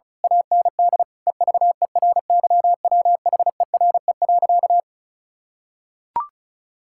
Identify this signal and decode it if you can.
Morse code, heard as a keyed tone, about 36 wpm, 700 Hz. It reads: AND EVERYWHERE.
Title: Sentences from Top 1600 Words 35wpm